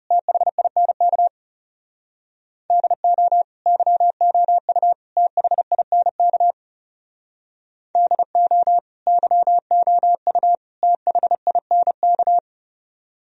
Morse code, heard as a keyed tone, about 40 wpm, 700 Hz. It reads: THINK DO YOU THINK DO YOU THINK